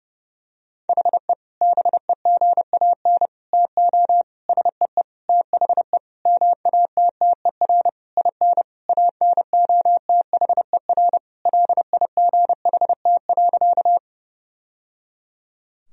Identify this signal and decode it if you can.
Morse code, heard as a keyed tone, about 30 wpm, 700 Hz. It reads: HE BEGAN TO SEE THE MATTER IN ANOTHER LIGHT.
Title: Sentences from Top 1000 Words - Review Part 1 - 30wpm